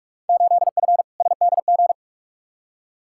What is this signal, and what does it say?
Morse code, heard as a keyed tone, about 46 wpm, 700 Hz. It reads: 8F SDG
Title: Field Day Exchanges Encore - 45wpm